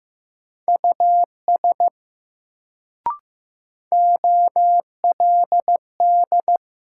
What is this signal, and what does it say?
Morse code, heard as a keyed tone, about 15 wpm, 700 Hz. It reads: US OLD
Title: Sentences from Top 200 Words - New Word Drill - 15wpm